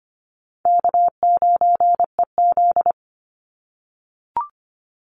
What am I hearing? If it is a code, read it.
Morse code, heard as a keyed tone, about 25 wpm, 700 Hz. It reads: K9EZ
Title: NA SOTA Chaser Callsigns 25wpm